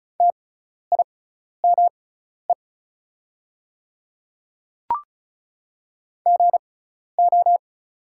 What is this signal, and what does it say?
Morse code, heard as a keyed tone, about 36 wpm, spaced as for 12 wpm, 690 Hz. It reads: TIME GO
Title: 6x Character Spacing - Top 200 Words - 35wpm